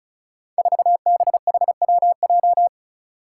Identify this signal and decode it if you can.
Morse code, heard as a keyed tone, about 36 wpm, 700 Hz. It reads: 4BHWJ
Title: Five Challenging Letter-Number Combinations 35wpm